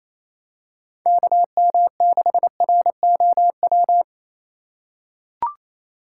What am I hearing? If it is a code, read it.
Morse code, heard as a keyed tone, about 28 wpm, 700 Hz. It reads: KM6ROW
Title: USA Call Signs - 6 Characters - 2x Character Spacing - 28wpm